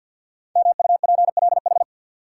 Morse code, heard as a keyed tone, about 50 wpm, 700 Hz. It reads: MUPLH
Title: Code Groups - 5 Letters - 50wpm